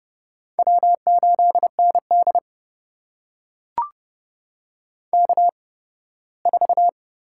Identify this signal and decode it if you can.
Morse code, heard as a keyed tone, about 30 wpm, spaced as for 18 wpm, 700 Hz. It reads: W8ND K4
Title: USA Call Signs - 4 Characters - 8x Character Spacing - 30wpm